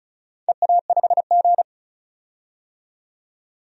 Morse code, heard as a keyed tone, about 36 wpm, 700 Hz. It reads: EA5G
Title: DXCC-100 Call Signs - 4 Characters - 35wpm